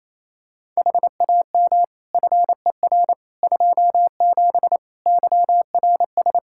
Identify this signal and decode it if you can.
Morse code, heard as a keyed tone, about 28 wpm, 700 Hz. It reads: HAM FER 27 YRS
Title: Practice QSO - Exchange 3 - More Info - 28wpm